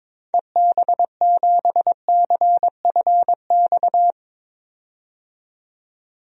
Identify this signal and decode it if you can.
Morse code, heard as a keyed tone, about 22 wpm, 700 Hz. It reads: EB7CFX
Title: DXCC-100 Call Signs - 6 Characters - 22wpm